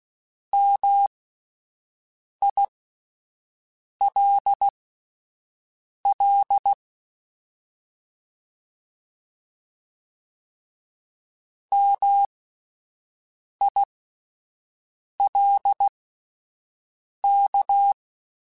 Morse code, heard as a keyed tone, about 16 wpm, spaced as for 5 wpm, 800 Hz. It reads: MILL MILK
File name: Alphabetical / Alphabetical mp3s / Ms @5